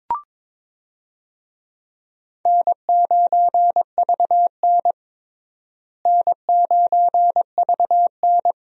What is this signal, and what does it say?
Morse code, heard as a keyed tone, about 22 wpm, 700 Hz. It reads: N9VN N9VN
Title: USA Call Signs - 4 Characters - Repeated 3x - 22wpm